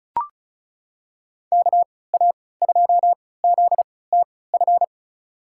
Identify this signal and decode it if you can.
Morse code, heard as a keyed tone, about 36 wpm, spaced as for 20 wpm, 700 Hz. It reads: KA2ZTF
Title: USA Call Signs - 6 Characters - 3x Character Spacing - 35wpm